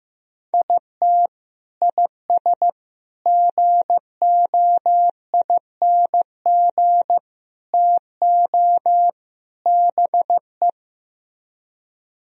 Morse code, heard as a keyed tone, about 15 wpm, 700 Hz. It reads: IT IS GOING TO BE
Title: Speed-Racing - Sets of 5 Words 15wpm